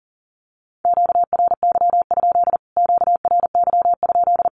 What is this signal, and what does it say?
Morse code, heard as a keyed tone, about 40 wpm, 700 Hz. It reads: QRY? QRY?
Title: Q-Codes Repeated 3x 40wpm